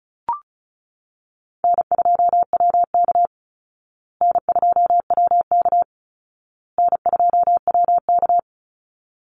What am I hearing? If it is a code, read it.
Morse code, heard as a keyed tone, about 34 wpm, 700 Hz. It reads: N2WK N2WK N2WK
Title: Call Signs Repeated 2x - 35wpm